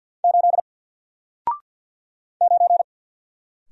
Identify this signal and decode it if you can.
Morse code, heard as a keyed tone, about 50 wpm, 700 Hz. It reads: 8 9
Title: Single Letter-Number Alt 50wpm